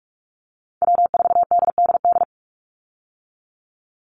Morse code, heard as a keyed tone, about 46 wpm, 700 Hz. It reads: W4DDD
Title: Call Signs - 45wpm